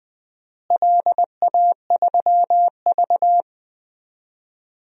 Morse code, heard as a keyed tone, about 21 wpm, 700 Hz. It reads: LA3V
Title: DXCC-100 Call Signs - 4 Characters - Repeated 2x - 20wpm